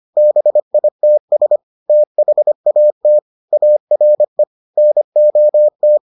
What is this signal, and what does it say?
Morse code, heard as a keyed tone, about 25 wpm, 600 Hz. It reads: BITS THAT ARE NOT